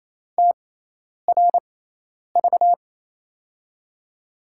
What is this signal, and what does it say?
Morse code, heard as a keyed tone, about 28 wpm, spaced as for 10 wpm, 700 Hz. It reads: TRV
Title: USA Call Signs - 6 Characters - 6x Character Spacing - 28wpm